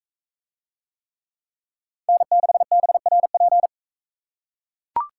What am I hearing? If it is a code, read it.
Morse code, heard as a keyed tone, about 40 wpm, 700 Hz. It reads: N6BRP